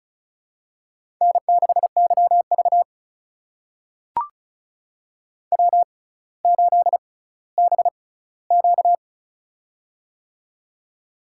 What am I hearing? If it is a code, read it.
Morse code, heard as a keyed tone, about 36 wpm, spaced as for 18 wpm, 700 Hz. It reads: N6YV W8BQ
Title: USA Call Signs - 4 Characters - 6x Character Spacing - 35wpm